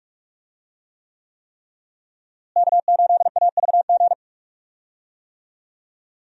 Morse code, heard as a keyed tone, about 46 wpm, 700 Hz. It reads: K8AVG
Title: USA Call Signs - 5 Characters - Repeated 3x - 45wpm